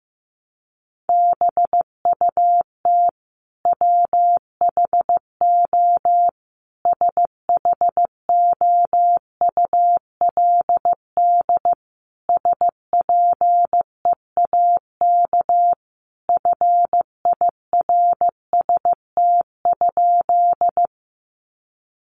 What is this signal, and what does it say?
Morse code, heard as a keyed tone, about 15 wpm, 700 Hz. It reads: BUT WHO SHOULD SPEAK FIRST?
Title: Sentences from Top 1100 Words 15wpm